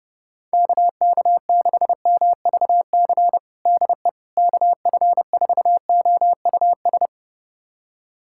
Morse code, heard as a keyed tone, about 30 wpm, 700 Hz. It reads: KK6MVC DE KF4OUS
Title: USA Callsign DE Callsign - 6 Characters - 30wpm